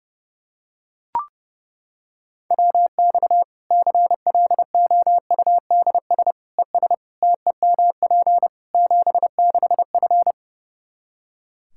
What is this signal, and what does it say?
Morse code, heard as a keyed tone, about 30 wpm, 700 Hz. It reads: WX CLOUDS ES TEMP 76F
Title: K1USN SST QSO Practice 30wpm